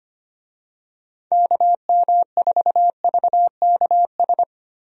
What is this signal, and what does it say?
Morse code, heard as a keyed tone, about 25 wpm, 700 Hz. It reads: KM4VKS
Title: USA Call Signs - 6 Characters - Repeated 2x - 25wpm